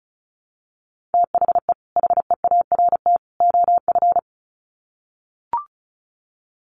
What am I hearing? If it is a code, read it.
Morse code, heard as a keyed tone, about 34 wpm, 700 Hz. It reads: THE HEART OF
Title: Sets of 3 Words 35wpm